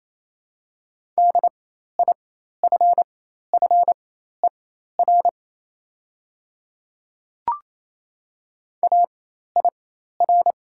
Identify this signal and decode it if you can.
Morse code, heard as a keyed tone, about 28 wpm, spaced as for 13 wpm, 700 Hz. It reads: DIFFER AIR
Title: 4x Character Spacing - Top 300 Words - 28wpm